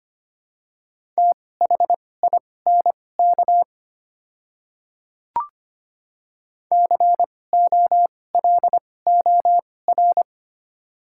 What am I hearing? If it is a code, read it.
Morse code, heard as a keyed tone, about 25 wpm, spaced as for 18 wpm, 700 Hz. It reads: THINK COLOR
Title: 2x Character Spacing - Top 300 Words - 25wpm